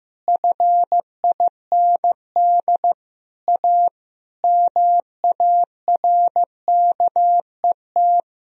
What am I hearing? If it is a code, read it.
Morse code, heard as a keyed tone, about 15 wpm, 700 Hz. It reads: FIND A MARKET
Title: Sentences from Top 1000 Words - Review Part 3 - 15wpm